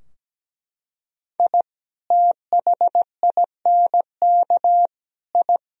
Morse code, heard as a keyed tone, about 17 wpm, 700 Hz. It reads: I THINK I
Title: Sentences from Top 1000 Words - Review Part 3 - 17wpm